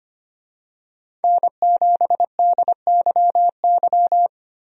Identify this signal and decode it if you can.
Morse code, heard as a keyed tone, about 25 wpm, 700 Hz. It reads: N7DYY
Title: USA Call Signs - 5 Characters - 25wpm